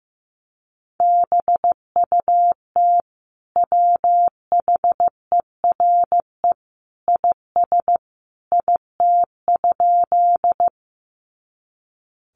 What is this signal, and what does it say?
Morse code, heard as a keyed tone, about 15 wpm, 700 Hz. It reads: BUT WHERE IS IT?
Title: Sentences from Top 200 Words 15wpm